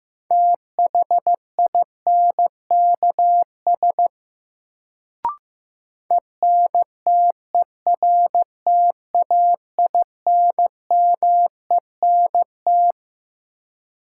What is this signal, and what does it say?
Morse code, heard as a keyed tone, about 15 wpm, 700 Hz. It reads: THINKS ENTERTAINMENT
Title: Sentences from Top 1000 Words - Review Part 6 - New Word Drill - 15wpm